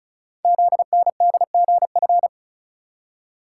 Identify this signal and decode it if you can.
Morse code, heard as a keyed tone, about 36 wpm, 700 Hz. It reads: ZNDGF
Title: Code Groups - 5 Letters - 35wpm